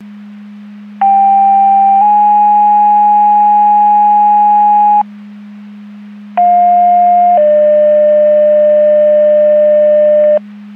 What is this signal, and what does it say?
Two-tone page, heard as a one-second tone, then a three-second tone, about 790 then 830 Hz, about 710 then 600 Hz. It is heard from 1.0 to 10.4 s.
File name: Portage 154.385 Paging